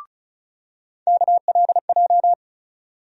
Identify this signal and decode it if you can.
Morse code, heard as a keyed tone, about 34 wpm, 700 Hz. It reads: KLJ
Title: USA Call Sign Postfixes - 3 Letters 35wpm